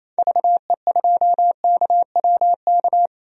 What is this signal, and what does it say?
Morse code, heard as a keyed tone, about 28 wpm, 700 Hz. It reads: VE2KWK
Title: DXCC-100 Call Signs - 6 Characters - 28wpm